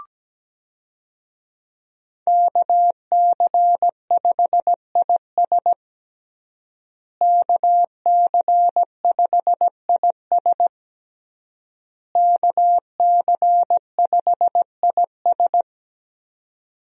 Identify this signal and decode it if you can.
Morse code, heard as a keyed tone, about 17 wpm, 700 Hz. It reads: KC5IS KC5IS KC5IS
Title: USA Call Signs - 5 Characters - Repeated 3x - 17wpm